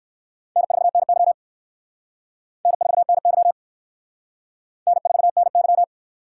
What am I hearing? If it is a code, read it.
Morse code, heard as a keyed tone, about 65 wpm, 700 Hz. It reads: N4NY N4NY N4NY